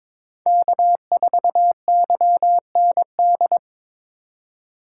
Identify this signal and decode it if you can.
Morse code, heard as a keyed tone, about 21 wpm, 700 Hz. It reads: K4YND
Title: K1USN SST QSO Practice - 22wpm